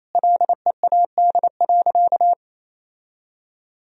Morse code, heard as a keyed tone, about 28 wpm, 700 Hz. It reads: LEAD.
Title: Sentences from Top 1000 Words - Review Part 2 - 28wpm